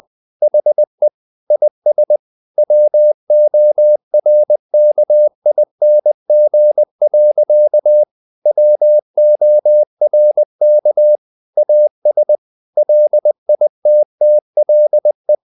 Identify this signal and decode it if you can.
Morse code, heard as a keyed tone, about 20 wpm, 600 Hz. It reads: HE IS WORKING. WORK AS LITTLE
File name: the old man and the sea / the old man and the sea part 1 20wpm